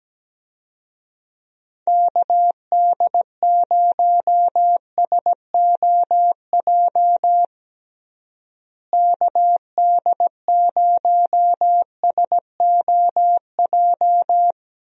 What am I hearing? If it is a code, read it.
Morse code, heard as a keyed tone, about 17 wpm, 700 Hz. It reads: KD0SOJ KD0SOJ
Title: USA Call Signs - 6 Characters - Repeated 2x - 17wpm